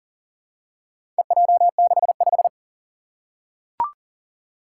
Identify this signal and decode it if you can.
Morse code, heard as a keyed tone, about 40 wpm, 700 Hz. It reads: EJ65
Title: Four Challenging Letter-Number Combinations 40wpm